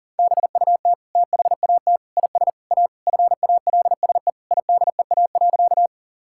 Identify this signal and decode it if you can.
Morse code, heard as a keyed tone, about 40 wpm, 700 Hz. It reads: BUT THAT IS A FALSE IDEA.
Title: Sentences from Top 1600 Words 40wpm